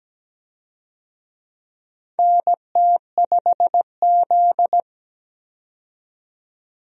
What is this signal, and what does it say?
Morse code, heard as a keyed tone, about 17 wpm, 700 Hz. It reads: NT5Z
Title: USA Call Signs - 4 Characters - 17wpm